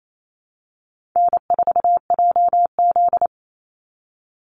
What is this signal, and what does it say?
Morse code, heard as a keyed tone, about 28 wpm, 700 Hz. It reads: N4JZ
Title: USA Call Signs 28wpm